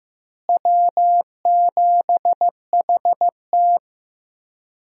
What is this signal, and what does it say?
Morse code, heard as a keyed tone, about 15 wpm, 700 Hz. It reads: W7HT
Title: USA Call Signs - 4 Characters - Repeated 3x - 15wpm